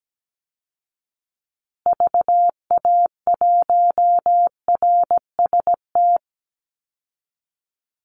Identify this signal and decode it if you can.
Morse code, heard as a keyed tone, about 17 wpm, 700 Hz. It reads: VA1RST
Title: Call Signs Repeated 3x - 17wpm